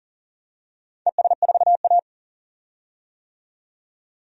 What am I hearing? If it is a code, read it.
Morse code, heard as a keyed tone, about 40 wpm, 690 Hz. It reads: ES4A
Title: DXCC-100 Call Signs - 4 Characters - 40wpm